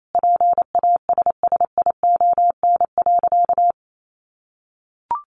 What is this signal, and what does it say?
Morse code, heard as a keyed tone, about 28 wpm, 700 Hz. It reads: PASSION.
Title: Sentences from Top 1300 Words 28wpm